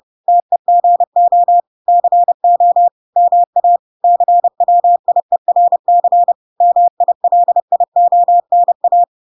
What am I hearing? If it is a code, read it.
Morse code, heard as a keyed tone, about 30 wpm, 700 Hz. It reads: TEGO CO MA CWIERC MILIONA